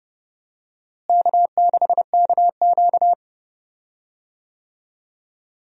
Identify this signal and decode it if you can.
Morse code, heard as a keyed tone, about 31 wpm, 700 Hz. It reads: K6KQ
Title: USA Call Signs - 4 Characters - Repeated 2x - 30wpm